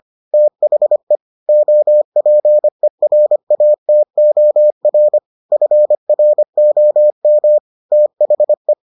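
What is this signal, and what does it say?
Morse code, heard as a keyed tone, about 25 wpm, 600 Hz. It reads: THE OPERATOR FROM THE
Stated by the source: Morse edition of the Foundations of Amateur Radio episodes.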